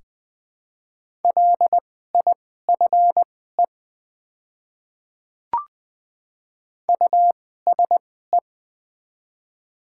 Morse code, heard as a keyed tone, about 20 wpm, spaced as for 15 wpm, 700 Hz. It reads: LIFE USE
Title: 2x Character Spacing - Top 300 Words - 20wpm